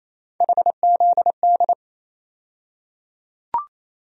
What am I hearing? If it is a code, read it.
Morse code, heard as a keyed tone, about 28 wpm, 700 Hz. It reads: HZD
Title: USA Call Sign Postfixes - 3 Letters 28wpm